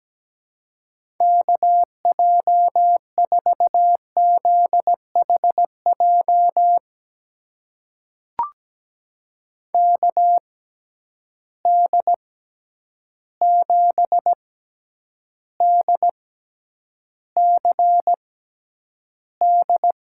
Morse code, heard as a keyed tone, about 17 wpm, spaced as for 9 wpm, 700 Hz. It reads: KJ4ZHJ KD7DCD
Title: USA Call Signs - 6 Characters - 6x Character Spacing - 17wpm